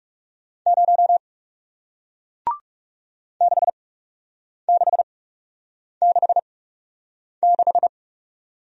Morse code, heard as a keyed tone, about 40 wpm, spaced as for 8 wpm, 700 Hz. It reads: 0 6666